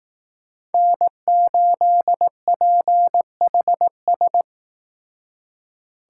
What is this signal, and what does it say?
Morse code, heard as a keyed tone, about 18 wpm, 700 Hz. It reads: N8PHS
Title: Speed Racing - USA 1x3 Call Signs 15wpm